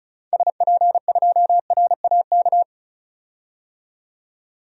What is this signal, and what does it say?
Morse code, heard as a keyed tone, about 36 wpm, 690 Hz. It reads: SP2RAK
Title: DXCC-100 Call Signs - 6 Characters - 35wpm